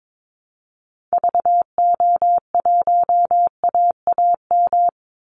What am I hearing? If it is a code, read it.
Morse code, heard as a keyed tone, about 22 wpm, 700 Hz. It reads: VO1AAM
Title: Call Signs Repeated 2x - 22wpm